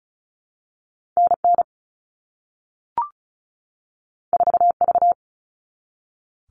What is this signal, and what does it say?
Morse code, heard as a keyed tone, about 36 wpm, 700 Hz. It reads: NN 4V
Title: All Cut Numbers - 2-Digit 35wpm